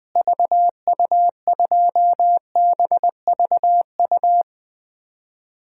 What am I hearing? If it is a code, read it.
Morse code, heard as a keyed tone, about 20 wpm, 700 Hz. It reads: VU2BVU
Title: DXCC-100 Call Signs - 6 Characters - Repeated 3x - 20wpm